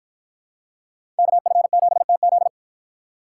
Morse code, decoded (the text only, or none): KF7TZ